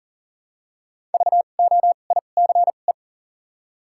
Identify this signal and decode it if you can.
Morse code, heard as a keyed tone, about 40 wpm, spaced as for 29 wpm, 700 Hz. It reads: VOICE
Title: 2x Character Spacing - Top 500 Words - 40wpm